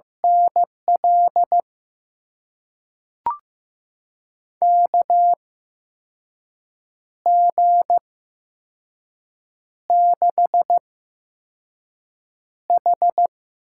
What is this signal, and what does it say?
Morse code, heard as a keyed tone, about 15 wpm, spaced as for 5 wpm, 700 Hz. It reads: NL KG6H
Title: USA Call Signs - 6 Characters - 8x Character Spacing - 15wpm